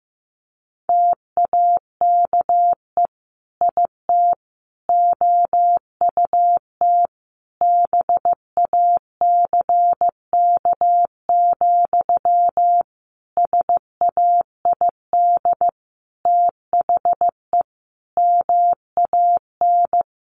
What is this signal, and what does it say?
Morse code, heard as a keyed tone, about 15 wpm, 700 Hz. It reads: TAKE IT OUT BACK, SAID THE MAN
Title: Sentences from Top 1500 Words 15wpm